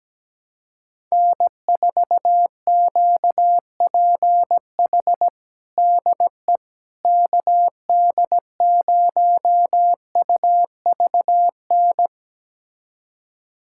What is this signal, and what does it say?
Morse code, heard as a keyed tone, about 17 wpm, 700 Hz. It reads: N4QPH DE KD0UVN
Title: USA Callsign DE Callsign 17wpm